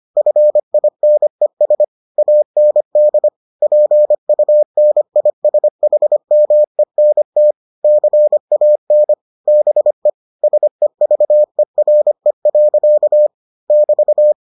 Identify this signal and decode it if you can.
Morse code, heard as a keyed tone, about 25 wpm, 600 Hz. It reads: FINES AND PUNISHMENT CAN BE SEVERE. <BT>